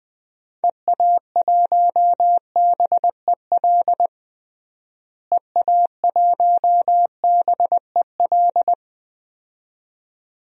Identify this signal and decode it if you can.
Morse code, heard as a keyed tone, about 20 wpm, 700 Hz. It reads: EA1BEL EA1BEL
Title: DXCC-100 Call Signs - 6 Characters - Repeated 2x - 20wpm